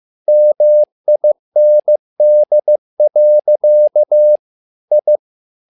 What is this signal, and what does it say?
Morse code, heard as a keyed tone, about 15 wpm, 600 Hz. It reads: MIND. I
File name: learning center / tower trouble / tower trouble 15wpm